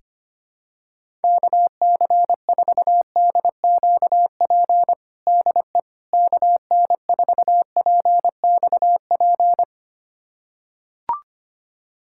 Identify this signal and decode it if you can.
Morse code, heard as a keyed tone, about 25 wpm, 700 Hz. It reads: KC4DQP DE KN4PXP
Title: USA Callsign DE Callsign 25wpm